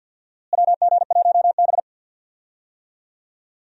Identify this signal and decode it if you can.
Morse code, heard as a keyed tone, about 50 wpm, 700 Hz. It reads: WG1B
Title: USA Call Signs - 4 Characters - Repeated 2x - 50wpm